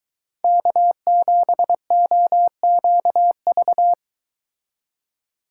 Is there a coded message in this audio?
Yes: Morse code from 0.4 to 3.9 s, heard as a keyed tone, about 23 wpm, 700 Hz. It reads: K7OQV